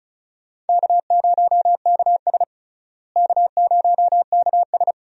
Morse code, heard as a keyed tone, about 36 wpm, 700 Hz. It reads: K0KS K0KS
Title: USA Call Signs - 4 Characters - Repeated 2x - 35wpm